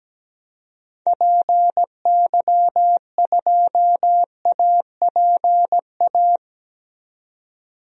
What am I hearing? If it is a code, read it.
Morse code, heard as a keyed tone, about 17 wpm, 700 Hz. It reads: PY2APA
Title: DXCC-100 Call Signs - 6 Characters - 17wpm